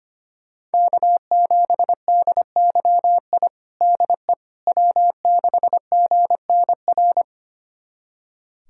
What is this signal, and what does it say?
Morse code, heard as a keyed tone, about 25 wpm, 700 Hz. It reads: K7DYI DE W6GNR
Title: USA Callsign DE Callsign - 5 Characters - 25wpm